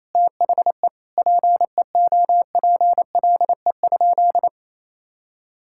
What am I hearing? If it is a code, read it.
Morse code, heard as a keyed tone, about 28 wpm, 700 Hz. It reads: THE PEOPLE?
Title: Sentences from Top 1000 Words - Review Part 5 - 28wpm